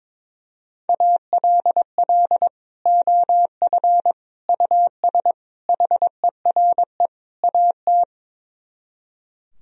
Morse code, heard as a keyed tone, about 22 wpm, 700 Hz. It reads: ALL OF US HERE AT
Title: Speed-Racing - Sets of 5 Words 22wpm